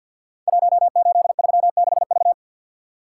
Morse code, heard as a keyed tone, about 52 wpm, 700 Hz. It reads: 1836V
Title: Code Groups - 5 Letter:Numbers - 50wpm